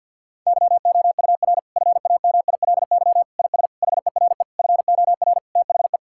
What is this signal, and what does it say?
Morse code, heard as a keyed tone, about 50 wpm, 700 Hz. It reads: YOUR FAMILY IS HERE FOR THE
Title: Sentences from Top 1600 Words 50wpm